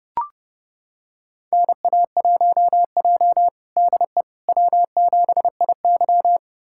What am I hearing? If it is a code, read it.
Morse code, heard as a keyed tone, about 30 wpm, 700 Hz. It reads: NA1J DE W7IY
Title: USA Callsign DE Callsign - 4 Characters - 30wpm